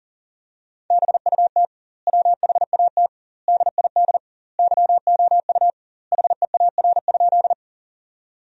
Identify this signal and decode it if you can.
Morse code, heard as a keyed tone, about 40 wpm, 700 Hz. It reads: BUT WHAT DID YOU HEAR?
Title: 2x Word-Spacing - Sentences from Top 300 Words - 40wpm